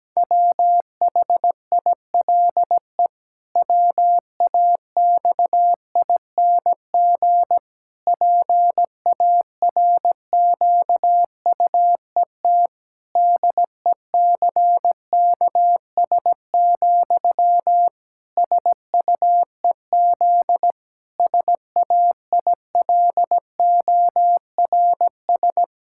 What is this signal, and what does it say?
Morse code, heard as a keyed tone, about 17 wpm, 700 Hz. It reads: WHILE WAXING PARQUET DECKS, SUEZ SAILORS
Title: Pangram Challenge v2 - 17wpm